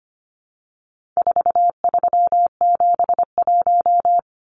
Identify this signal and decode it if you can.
Morse code, heard as a keyed tone, about 25 wpm, 700 Hz. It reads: 4371
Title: Numbers - 4-Digit 25wpm